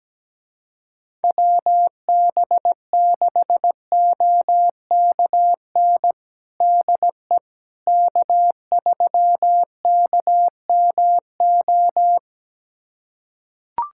Morse code, heard as a keyed tone, about 17 wpm, 700 Hz. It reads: WB6OKN DE K3KMO
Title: USA Callsign DE Callsign 17wpm